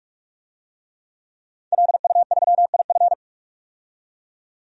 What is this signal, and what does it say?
Morse code, heard as a keyed tone, about 46 wpm, 700 Hz. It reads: LU3IF
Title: DXCC-100 Call Signs - 5 Characters - 45wpm